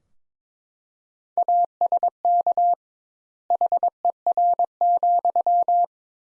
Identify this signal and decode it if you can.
Morse code, heard as a keyed tone, about 22 wpm, 700 Hz. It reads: ASK HER,
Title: Sentences Top 200 Words - 22wpm